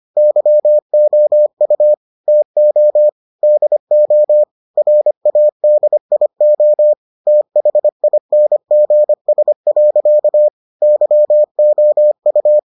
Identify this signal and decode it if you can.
Morse code, heard as a keyed tone, about 25 wpm, 600 Hz. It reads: YOU TO DO RADIO THINGS. YOU